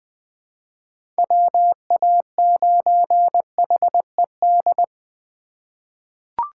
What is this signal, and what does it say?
Morse code, heard as a keyed tone, about 20 wpm, 700 Hz. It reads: WA9HED
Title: 2023 SOTA Chaser Call Signs - 20wpm